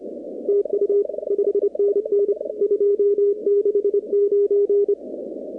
Morse code, heard as a keyed tone, about 29 wpm, 400 Hz. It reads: TU 5NN 269
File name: contests / 2019-cw-wpx-kt5j-k5pi- -2330 0-41